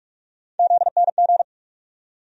Morse code, heard as a keyed tone, about 46 wpm, 700 Hz. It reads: ZNG
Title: USA Call Sign Postfixes - 3 Letters 45wpm